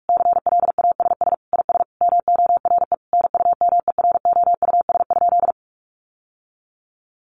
Morse code, heard as a keyed tone, about 44 wpm, 700 Hz. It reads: CLASS IS MORE NUMEROUS?
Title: Sentences from Top 1400 Words 45wpm